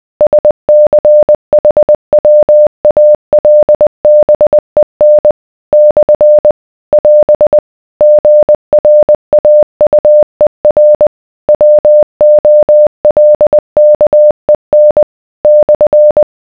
Morse code, heard as a keyed tone, about 20 wpm, 600 Hz. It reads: SCHWALBEN / <AS> GRAUER WOLKEN /